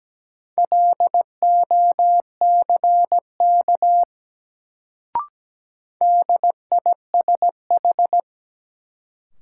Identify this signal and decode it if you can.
Morse code, heard as a keyed tone, about 17 wpm, 700 Hz. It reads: LOCK DISH
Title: Sentences from Top 1300 Words - New Word Drill Alt - 17wpm